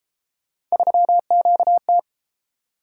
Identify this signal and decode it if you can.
Morse code, heard as a keyed tone, about 30 wpm, 700 Hz. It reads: 3QT